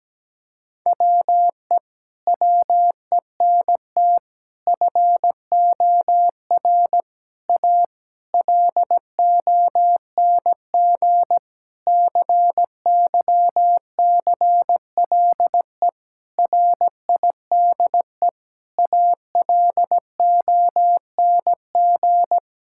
Morse code, heard as a keyed tone, about 17 wpm, 700 Hz. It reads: WE WENT FOR A LONG CYCLE RIDE ALONG
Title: Sentences from Top 1000 Words - Review Part 6 - 17wpm